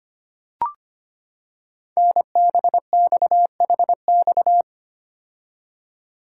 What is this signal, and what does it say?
Morse code, heard as a keyed tone, about 25 wpm, 700 Hz. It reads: NBXHX
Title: Code Groups - 5 Letters - 25wpm